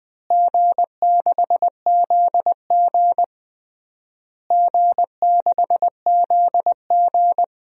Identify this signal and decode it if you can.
Morse code, heard as a keyed tone, about 21 wpm, 700 Hz. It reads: G6ZG G6ZG
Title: DXCC-100 Call Signs - 4 Characters - Repeated 3x - 20wpm